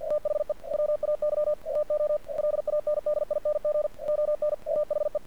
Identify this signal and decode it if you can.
Morse code, heard as a keyed tone, about 49 wpm, 600 Hz. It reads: THE WAY TO LANDING ON AHE
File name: archived / CRSnet